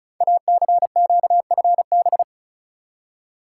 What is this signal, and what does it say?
Morse code, heard as a keyed tone, about 36 wpm, 690 Hz. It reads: ACQFB
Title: Code Groups - 5 Letters - 35wpm